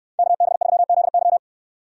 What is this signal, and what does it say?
Morse code, heard as a keyed tone, about 68 wpm, 700 Hz. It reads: KB3ZY